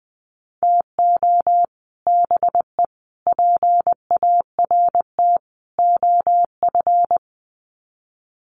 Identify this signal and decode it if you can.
Morse code, heard as a keyed tone, about 20 wpm, 700 Hz. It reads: TO BE PART OF
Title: Sets of 4 Words 20wpm